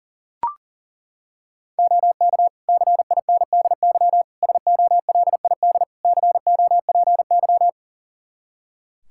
Morse code, heard as a keyed tone, about 40 wpm, 700 Hz. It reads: OK CINDY SOLID COPY
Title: K1USN SST QSO Practice 40wpm